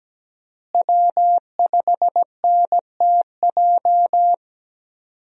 Morse code, heard as a keyed tone, about 17 wpm, 700 Hz. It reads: W5NTJ
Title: USA Call Signs - 5 Characters - Repeated 3x - 17wpm